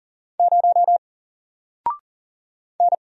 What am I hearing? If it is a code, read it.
Morse code, heard as a keyed tone, about 38 wpm, 700 Hz. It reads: 0 N
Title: Single Letter-Number Alt 40wpm